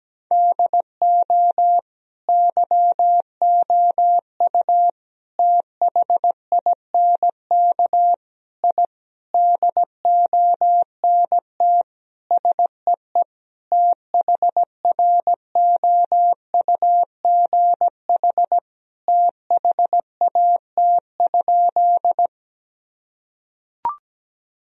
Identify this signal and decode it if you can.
Morse code, heard as a keyed tone, about 17 wpm, 700 Hz. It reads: DO YOU THINK I DONT SEE THROUGH THAT?
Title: Sentences from Top 1000 Words - Review Part 1 - 17wpm